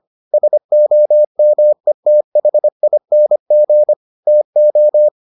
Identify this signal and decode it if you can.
Morse code, heard as a keyed tone, about 25 wpm, 600 Hz. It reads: SOMETHING TO